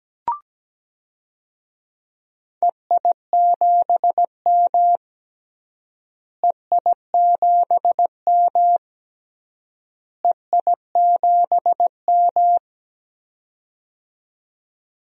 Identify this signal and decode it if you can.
Morse code, heard as a keyed tone, about 17 wpm, 700 Hz. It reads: EI7M EI7M EI7M
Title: DXCC-100 Call Signs - 4 Characters - Repeated 3x - 17wpm